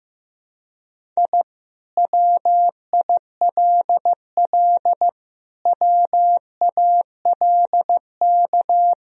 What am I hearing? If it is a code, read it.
Morse code, heard as a keyed tone, about 15 wpm, 700 Hz. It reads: I WILL WALK
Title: Sentences from Top 1000 Words - Review Part 1 - 15wpm